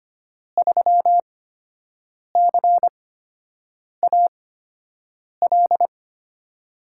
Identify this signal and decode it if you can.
Morse code, heard as a keyed tone, about 25 wpm, spaced as for 7 wpm, 700 Hz. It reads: 3CAL
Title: USA Call Signs - 5 Characters - 8x Character Spacing - 25wpm